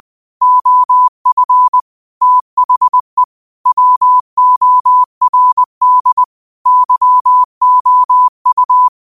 Morse code, heard as a keyed tone, about 20 wpm, 1000 Hz. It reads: OF THE WORD YOU